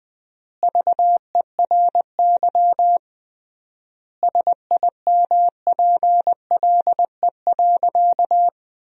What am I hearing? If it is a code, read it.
Morse code, heard as a keyed tone, about 20 wpm, 700 Hz. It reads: VERY SIMPLE.
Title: 3x Word-Spacing - Sentences from Top 500 Words - 20wpm